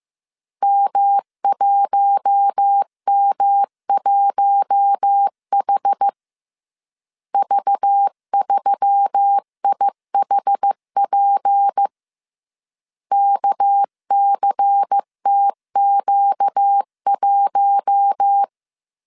Morse code, heard as a keyed tone, about 15 wpm, 800 Hz. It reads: M1M1H V3IHP KCTQ1
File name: CW / DET / 1 6 1500C S B 14wpm 2958